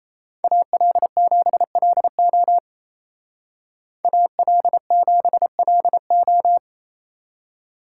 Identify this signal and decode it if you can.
Morse code, heard as a keyed tone, about 34 wpm, 700 Hz. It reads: AL7LO AL7LO
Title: Speed Racing - USA 2x2 Call Signs 28wpm